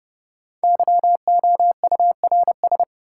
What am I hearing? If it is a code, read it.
Morse code, heard as a keyed tone, about 32 wpm, 700 Hz. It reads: YOURS
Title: Sentences from Top 1500 Words - New Word Drill - 30wpm